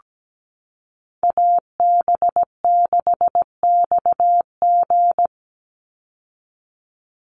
Morse code, heard as a keyed tone, about 17 wpm, 700 Hz. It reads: AB6XG
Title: Call Signs - 17wpm